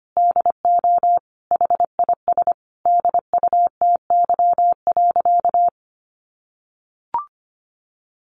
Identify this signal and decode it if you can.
Morse code, heard as a keyed tone, about 25 wpm, 700 Hz. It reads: DO HIS DUTY.
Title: Sentences from Top 1500 Words 25wpm